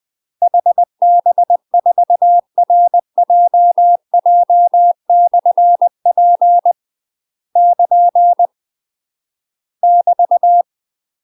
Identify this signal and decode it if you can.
Morse code, heard as a keyed tone, about 20 wpm, 700 Hz. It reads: HB4RJJ/P <KN> <BT>
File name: UK MASTER QSOs TRAINING / 20WPM / UK MASTER QSOs 0011